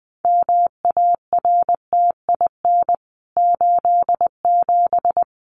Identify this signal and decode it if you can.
Morse code, heard as a keyed tone, about 20 wpm, 700 Hz. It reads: MARTIN 87
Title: CWT Exchange 20wpm